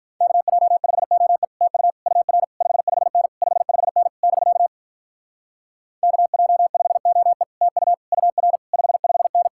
Morse code, heard as a keyed tone, about 46 wpm, 700 Hz. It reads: KJ5OE TU UR 55N 55N <BK> KJ5OE TU UR 55N